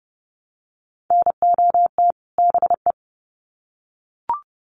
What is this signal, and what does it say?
Morse code, heard as a keyed tone, about 30 wpm, 700 Hz. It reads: NOT BE
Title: Sets of 2 Words 30wpm